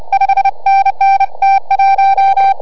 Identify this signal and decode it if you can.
Morse code, heard as a keyed tone, about 30 wpm, 690 Hz, with noise. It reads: 5NN T1